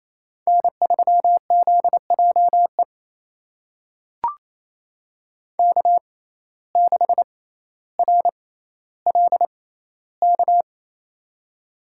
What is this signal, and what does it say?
Morse code, heard as a keyed tone, about 27 wpm, spaced as for 14 wpm, 700 Hz. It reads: N3ZJE K6RLK
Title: USA Call Signs - 5 Characters - 6x Character Spacing - 28wpm